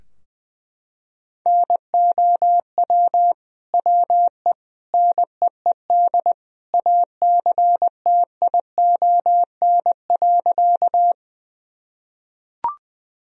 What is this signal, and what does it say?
Morse code, heard as a keyed tone, about 20 wpm, 700 Hz. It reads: NOW WE NEED ACTION.
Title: Sentences from Top 1000 Words - Review Part 4 - 20wpm